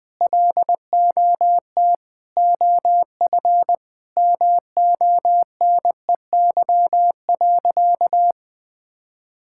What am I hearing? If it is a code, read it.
Morse code, heard as a keyed tone, about 21 wpm, 700 Hz. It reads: LOT OF MONEY.
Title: Sentences from Top 1000 Words - Review Part 6 - 20wpm